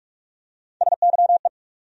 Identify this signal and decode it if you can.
Morse code, heard as a keyed tone, about 46 wpm, 700 Hz. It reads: SYE
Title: USA Call Sign Postfixes - 3 Letters 45wpm